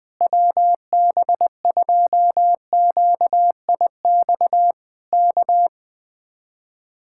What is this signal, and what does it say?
Morse code, heard as a keyed tone, about 20 wpm, 700 Hz. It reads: WB2QIX K
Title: Speed-Racing Common QSO Phrases Encore - 20wpm